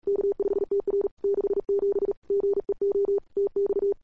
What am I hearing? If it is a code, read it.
Morse code, heard as a keyed tone, about 39 wpm, 400 Hz. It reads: K5TR 67 GEO TX
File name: cw-sprint-k5tr-op-n5rz